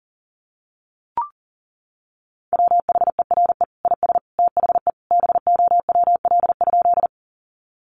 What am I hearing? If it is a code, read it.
Morse code, heard as a keyed tone, about 40 wpm, 700 Hz. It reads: WHERE IS THE BOWL?
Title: Sentences from Top 1200 Words 40wpm